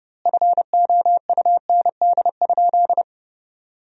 Morse code, heard as a keyed tone, about 30 wpm, 700 Hz. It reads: FOUND?
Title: 3x Word-Spacing - Sentences from Top 300 Words - 30wpm